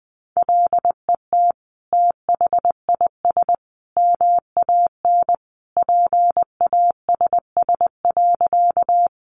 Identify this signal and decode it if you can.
Morse code, heard as a keyed tone, about 20 wpm, 700 Hz. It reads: LET THIS MAN PASS.
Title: Sentences from Top 400 Words 20wpm